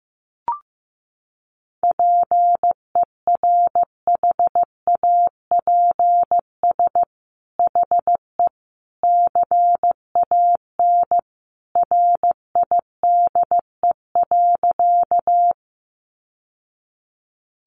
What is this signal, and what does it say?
Morse code, heard as a keyed tone, about 15 wpm, 700 Hz. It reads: PERHAPS HE CAN RIDE.
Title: Sentences from Top 1200 Words 15wpm